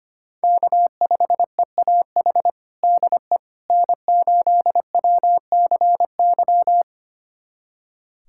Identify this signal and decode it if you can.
Morse code, heard as a keyed tone, about 25 wpm, 700 Hz. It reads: K5EAH DE N8WCY
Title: USA Callsign DE Callsign - 5 Characters - 25wpm